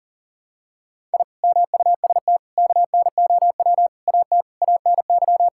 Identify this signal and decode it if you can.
Morse code, heard as a keyed tone, about 37 wpm, 700 Hz. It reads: I MUST KNOW AT ANY
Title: Sentences from Top 1000 Words - Review Part 5 - 40wpm